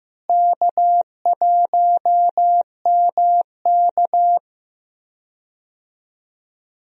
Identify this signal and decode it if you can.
Morse code, heard as a keyed tone, about 15 wpm, 700 Hz. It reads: K1MK
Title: USA Call Signs - 4 Characters - Repeated 3x - 15wpm